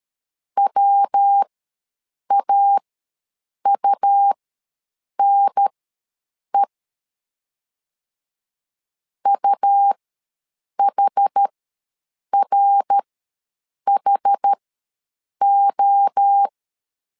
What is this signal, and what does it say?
Morse code, heard as a keyed tone, about 13 wpm, spaced as for 7 wpm, 800 Hz. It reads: WAUNE UHRHO
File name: LEZIONI / 3 MWRO / 3 MWRO ws 12wpm 1914